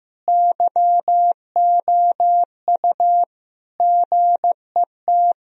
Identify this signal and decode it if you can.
Morse code, heard as a keyed tone, about 16 wpm, 700 Hz. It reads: YOU GET
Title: Sentences from Top 1000 Words - Review Part 4 - 15wpm